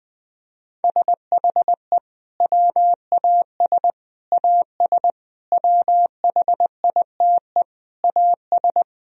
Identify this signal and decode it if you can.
Morse code, heard as a keyed tone, about 21 wpm, 700 Hz. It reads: SHE WAS AS WHITE AS
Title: Sentences from Top 1000 Words - Review Part 3 - 20wpm